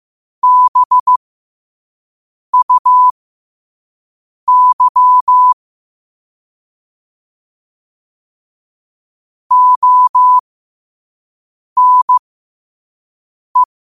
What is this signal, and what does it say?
Morse code, heard as a keyed tone, about 15 wpm, spaced as for 5 wpm, 1000 Hz. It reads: BUY ONE
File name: qotdfiles / 5wpm